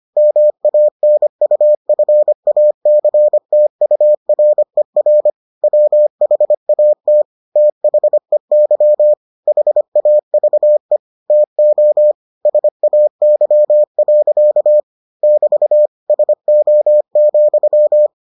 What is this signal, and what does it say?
Morse code, heard as a keyed tone, about 25 wpm, 600 Hz. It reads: MANUFACTURER WHAT THEY HAVE TO SAY. <BT> SO,